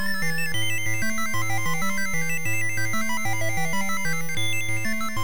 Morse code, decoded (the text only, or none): none